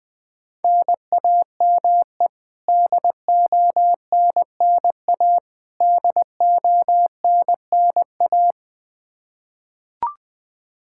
Morse code, heard as a keyed tone, about 21 wpm, 700 Hz. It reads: NAME DONNA DONNA
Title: Practice QSO - Exchange 1 - Beginning - 20wpm